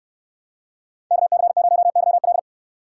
Morse code, heard as a keyed tone, about 66 wpm, 700 Hz. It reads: KC0YL